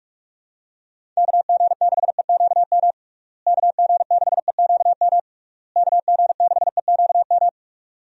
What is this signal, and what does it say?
Morse code, heard as a keyed tone, about 46 wpm, 700 Hz. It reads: KG6EQM KG6EQM KG6EQM
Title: USA Call Signs - 6 Characters - Repeated 3x - 45wpm